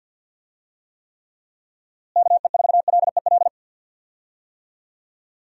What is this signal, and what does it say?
Morse code, heard as a keyed tone, about 48 wpm, 700 Hz. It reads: KE4LEL
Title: USA Call Signs - 6 Characters - Repeated 3x - 50wpm